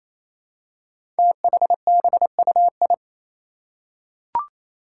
Morse code, heard as a keyed tone, about 27 wpm, 700 Hz. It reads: THBUI
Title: Five Challenging Letter-Number Combinations 28wpm